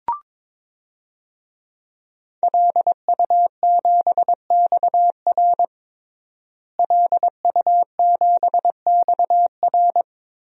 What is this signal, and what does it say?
Morse code, heard as a keyed tone, about 22 wpm, 700 Hz. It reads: LU7XR LU7XR
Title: DXCC-100 Call Signs - 5 Characters - Repeated 3x - 22wpm